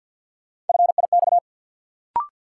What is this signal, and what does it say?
Morse code, heard as a keyed tone, about 47 wpm, 700 Hz. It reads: FIX
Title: Sentences from Top 1400 Words - New Word Drill Alt - 50wpm